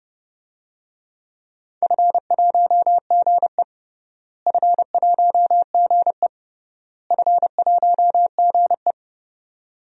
Morse code, heard as a keyed tone, about 31 wpm, 700 Hz. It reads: F1GE F1GE F1GE
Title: DXCC-100 Call Signs - 4 Characters - Repeated 3x - 30wpm